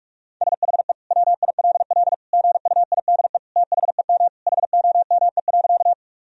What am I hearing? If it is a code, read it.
Morse code, heard as a keyed tone, about 44 wpm, 700 Hz. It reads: SHE WILL GUIDE THEM HOME.
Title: Sentences from Top 1000 Words - Review Part 3 - 45wpm